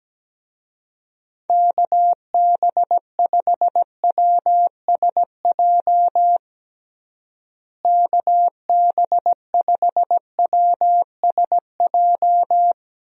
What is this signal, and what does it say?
Morse code, heard as a keyed tone, about 17 wpm, 700 Hz. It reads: KB5WSJ KB5WSJ
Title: USA Call Signs - 6 Characters - Repeated 3x - 17wpm